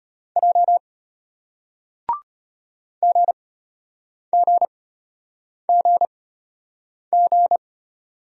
Morse code, heard as a keyed tone, about 34 wpm, spaced as for 8 wpm, 700 Hz. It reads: J GGGG